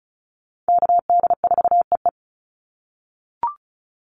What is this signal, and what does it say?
Morse code, heard as a keyed tone, about 34 wpm, 700 Hz. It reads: KD4EE
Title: Call Signs - 35wpm